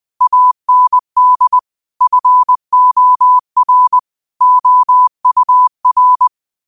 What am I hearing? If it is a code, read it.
Morse code, heard as a keyed tone, about 20 wpm, 1000 Hz. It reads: AND FOR OUR
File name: qotdfiles / 20wpm